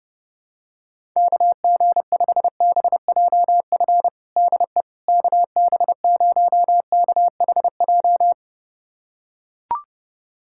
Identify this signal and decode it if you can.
Morse code, heard as a keyed tone, about 30 wpm, 700 Hz. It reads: KG5BJF DE KB0KHJ
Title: USA Callsign DE Callsign - 6 Characters - 30wpm